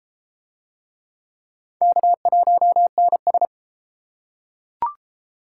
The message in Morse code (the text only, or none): K1NS